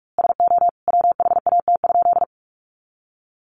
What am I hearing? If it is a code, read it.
Morse code, heard as a keyed tone, about 44 wpm, 700 Hz. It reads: SO WHAT?
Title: A Sound Of Thunder 45wpm